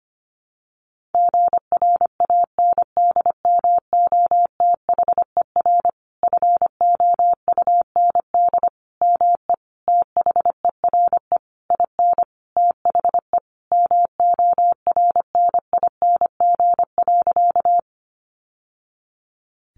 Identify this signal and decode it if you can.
Morse code, heard as a keyed tone, about 25 wpm, 700 Hz. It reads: GRANDMOTHER FOUND ME THERE IN THE MORNING.
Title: Sentences from Top 1200 Words 25wpm